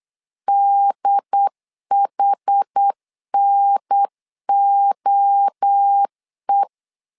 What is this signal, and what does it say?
Morse code, heard as a keyed tone, about 8 wpm, 800 Hz. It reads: DHNOE
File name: LEZIONI / 4 Q1S 8wpm 2710